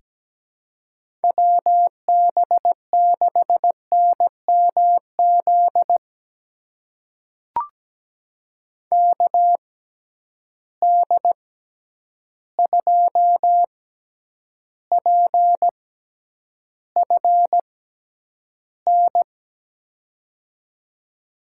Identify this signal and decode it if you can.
Morse code, heard as a keyed tone, about 17 wpm, spaced as for 9 wpm, 700 Hz. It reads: WB6NMZ KD2PFN
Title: USA Call Signs - 6 Characters - 6x Character Spacing - 17wpm